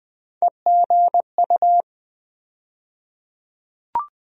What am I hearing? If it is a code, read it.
Morse code, heard as a keyed tone, about 20 wpm, 700 Hz. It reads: EGU
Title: USA Call Sign Postfixes - 3 Letters 20wpm